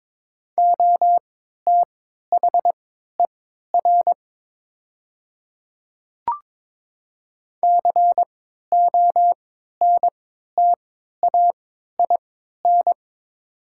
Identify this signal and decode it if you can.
Morse code, heard as a keyed tone, about 22 wpm, spaced as for 13 wpm, 700 Hz. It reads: OTHER CONTAIN
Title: 3x Character Spacing - Top 500 Words - 22wpm